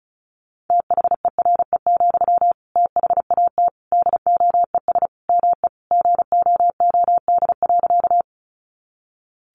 Morse code, heard as a keyed tone, about 34 wpm, 700 Hz. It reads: THERE, THAT DOES ME GOOD.
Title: Sentences from Top 200 Words 35wpm